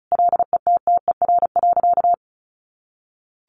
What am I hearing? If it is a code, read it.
Morse code, heard as a keyed tone, about 34 wpm, 700 Hz. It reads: LETTER.
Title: Sentences from Top 1400 Words 35wpm